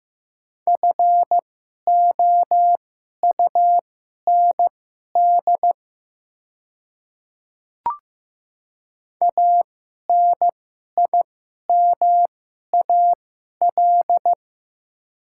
Morse code, heard as a keyed tone, about 15 wpm, spaced as for 11 wpm, 690 Hz. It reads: FOUND ANIMAL
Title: 2x Character Spacing - Top 300 Words - 15wpm